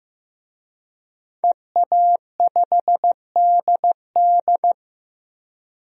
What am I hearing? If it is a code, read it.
Morse code, heard as a keyed tone, about 16 wpm, 700 Hz. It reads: EA5DD
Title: DXCC-100 Call Signs - 5 Characters - 15wpm